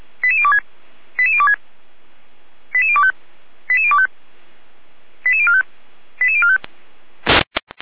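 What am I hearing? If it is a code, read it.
Five-tone selective call (ZVEI), heard as quick runs of five tones, tones of about 70 ms. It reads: 88026 88026 88025 88025 88045 88045
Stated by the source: ZVEI five-tone call 88103 received on SDR, Feuerwehr Orlamünde.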